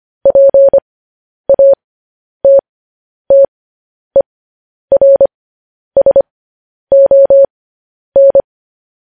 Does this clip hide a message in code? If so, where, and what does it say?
Yes: Morse code from 0.3 to 8.4 s, heard as a keyed tone, about 25 wpm, spaced as for 10 wpm, 550 Hz. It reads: PATTERSON